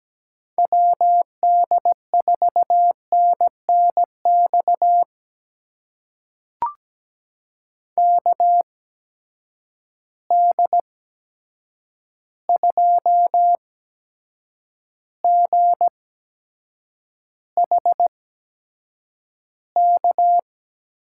Morse code, heard as a keyed tone, about 17 wpm, spaced as for 7 wpm, 700 Hz. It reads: WD4NNX KD2GHK
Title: USA Call Signs - 6 Characters - 8x Character Spacing - 17wpm